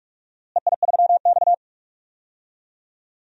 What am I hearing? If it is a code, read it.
Morse code, heard as a keyed tone, about 46 wpm, 700 Hz. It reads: EI3X
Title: DXCC-100 Call Signs - 4 Characters - Repeated 2x - 45wpm